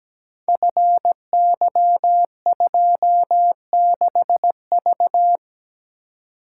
Morse code, heard as a keyed tone, about 17 wpm, 700 Hz. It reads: FY26V
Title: Code Groups - 5 Letter:Numbers - 17wpm